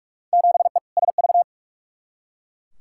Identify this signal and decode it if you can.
Morse code, heard as a keyed tone, about 41 wpm, 700 Hz. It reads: 7E SV
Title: Field Day Exchanges Encore - 45wpm